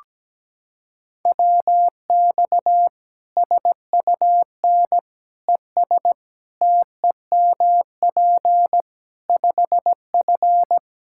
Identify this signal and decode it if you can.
Morse code, heard as a keyed tone, about 17 wpm, 700 Hz. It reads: WX SUN ES TEMP 5F
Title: Practice QSO - Exchange 2 - Info - 17wpm